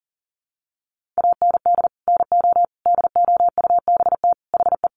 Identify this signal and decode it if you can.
Morse code, heard as a keyed tone, about 40 wpm, 700 Hz. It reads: AND NO DOUBT HE
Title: Sentences from Top 1500 Words 40wpm